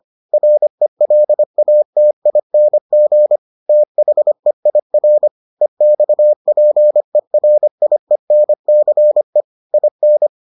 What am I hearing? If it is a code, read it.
Morse code, heard as a keyed tone, about 25 wpm, 600 Hz. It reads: RELATING THEIR EXPERIENCE IN